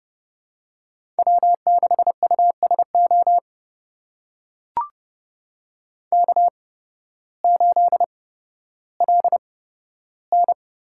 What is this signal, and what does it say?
Morse code, heard as a keyed tone, about 30 wpm, spaced as for 14 wpm, 700 Hz. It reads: W6USO K8LN
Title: USA Call Signs - 5 Characters - 8x Character Spacing - 30wpm